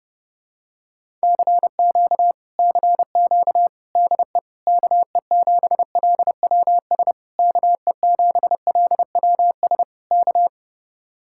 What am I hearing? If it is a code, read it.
Morse code, heard as a keyed tone, about 31 wpm, 700 Hz. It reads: CQ CQ DE KE7LWS KE7LWS K